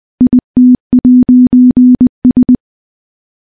Morse code, heard as a keyed tone, about 20 wpm, 250 Hz. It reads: IT'S
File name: morse / morse-training-20WPM